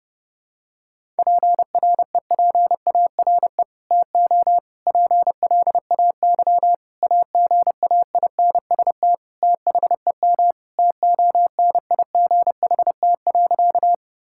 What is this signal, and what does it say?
Morse code, heard as a keyed tone, about 28 wpm, 700 Hz. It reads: PREPARE TO PLAY AGAINST THEM TONIGHT.
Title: Sentences from Top 1000 Words - Review Part 6 - 30wpm